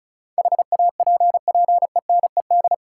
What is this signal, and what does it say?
Morse code, heard as a keyed tone, about 36 wpm, 700 Hz. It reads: HAPPENED
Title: Sentences from Top 1000 Words - Review Part 2 - New Word Drill - 35wpm